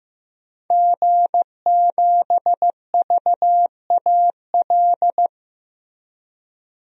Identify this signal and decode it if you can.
Morse code, heard as a keyed tone, about 15 wpm, 700 Hz. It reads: G7VAL
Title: DXCC-100 Call Signs - 5 Characters - Repeated 2x - 15wpm